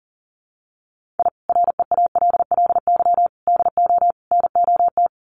Morse code, heard as a keyed tone, about 40 wpm, 700 Hz. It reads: I REALLY DO NOT
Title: Sets of 4 Words 40wpm